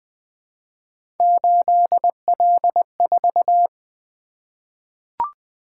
Morse code, heard as a keyed tone, about 20 wpm, 700 Hz. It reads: 8L4
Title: Three Letter-Number Combinations - 20wpm